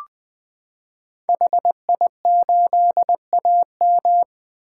Morse code, heard as a keyed tone, about 21 wpm, 700 Hz. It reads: HI8AM
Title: 2023 POTA Activator Call Signs - 20wpm